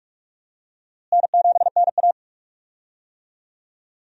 Morse code, heard as a keyed tone, about 44 wpm, 700 Hz. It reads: N7NA
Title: USA Call Signs - 4 Characters - Repeated 3x - 45wpm